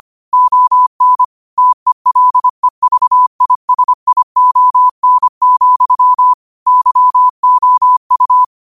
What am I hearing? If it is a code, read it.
Morse code, heard as a keyed tone, about 25 wpm, 1000 Hz. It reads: ON TELEVISION, YOU